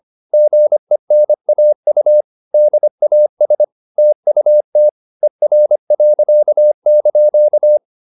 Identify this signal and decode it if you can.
Morse code, heard as a keyed tone, about 25 wpm, 600 Hz. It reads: GENAU DAS TUT ER.)